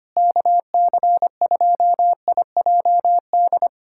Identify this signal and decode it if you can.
Morse code, heard as a keyed tone, about 25 wpm, 700 Hz. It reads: KC2IJD
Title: USA Callsign DE Callsign - 6 Characters - 25wpm